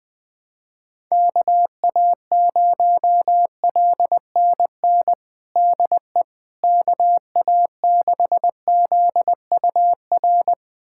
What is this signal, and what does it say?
Morse code, heard as a keyed tone, about 20 wpm, 700 Hz. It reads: KA0LNN DE KA6ZUR
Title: USA Callsign DE Callsign - 6 Characters - 20wpm